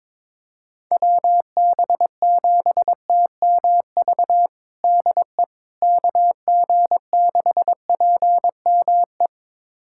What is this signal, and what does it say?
Morse code, heard as a keyed tone, about 22 wpm, 700 Hz. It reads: WB7TMV DE KG6PME
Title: USA Callsign DE Callsign 22wpm